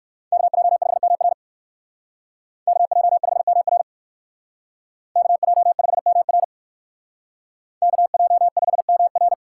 Morse code, heard as a keyed tone, about 59 wpm, 700 Hz. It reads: KJ5MR KJ5MR KJ5MR KJ5MR